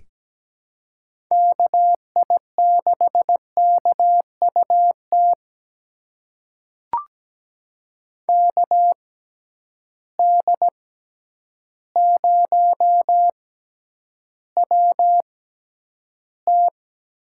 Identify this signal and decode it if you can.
Morse code, heard as a keyed tone, about 17 wpm, spaced as for 9 wpm, 700 Hz. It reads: KI6KUT KD0WT
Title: USA Call Signs - 6 Characters - 6x Character Spacing - 17wpm